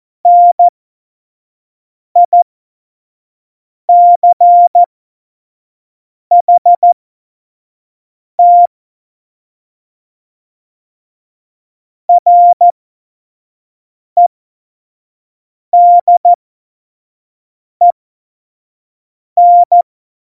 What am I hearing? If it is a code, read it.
Morse code, heard as a keyed tone, about 14 wpm, spaced as for 5 wpm, 700 Hz. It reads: NICHT REDEN
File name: Morsekurs / Witze / Witz 16. Oktober 2018, 25 F70